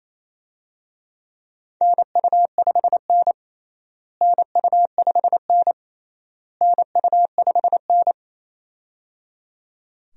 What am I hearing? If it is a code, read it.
Morse code, heard as a keyed tone, about 28 wpm, 700 Hz. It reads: NU5N NU5N NU5N
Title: USA Call Signs - 4 Characters - Repeated 3x - 28wpm